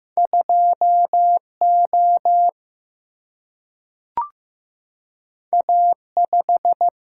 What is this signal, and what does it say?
Morse code, heard as a keyed tone, about 15 wpm, 700 Hz. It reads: 2 O A 5
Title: Single-Letter Single-Number 15wpm